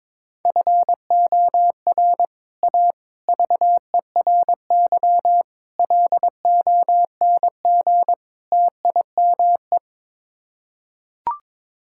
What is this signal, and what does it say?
Morse code, heard as a keyed tone, about 22 wpm, 700 Hz. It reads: FOR A VERY LONG TIME
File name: practice / 22wpm / Build A Phrase - 5-Word Phrases - 22wpm